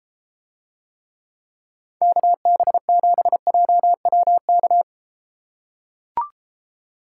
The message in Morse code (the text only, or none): KB7JWK